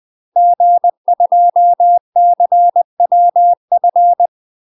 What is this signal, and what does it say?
Morse code, heard as a keyed tone, about 20 wpm, 700 Hz. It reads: G2CWF
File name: UK MASTER QSOs TRAINING / 20WPM / UK MASTER QSOs 0029